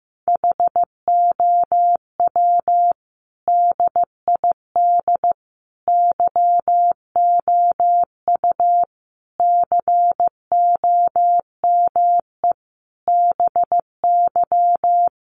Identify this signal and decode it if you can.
Morse code, heard as a keyed tone, about 15 wpm, 700 Hz. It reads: HOW DID YOU COME BY
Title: Sentences from Top 100 Words 15wpm